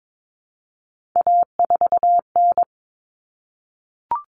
Morse code, heard as a keyed tone, about 22 wpm, 700 Hz. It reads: A4N
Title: Common Cut Numbers - 3-Digit 22wpm